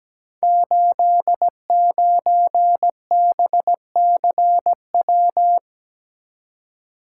Morse code, heard as a keyed tone, about 17 wpm, 700 Hz. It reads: 89BCW
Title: Code Groups - 5 Letter:Numbers - 17wpm